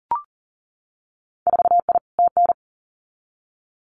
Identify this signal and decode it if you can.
Morse code, heard as a keyed tone, about 40 wpm, 700 Hz. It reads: 4I TN
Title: Winter Field Day Exchanges 40wpm